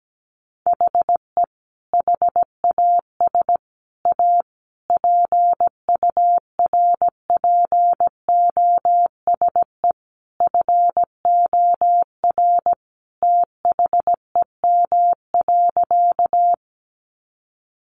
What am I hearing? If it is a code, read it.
Morse code, heard as a keyed tone, about 17 wpm, 700 Hz. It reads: HE HAS A PURPOSE FOR THEM.
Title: Sentences from Top 800 Words 17wpm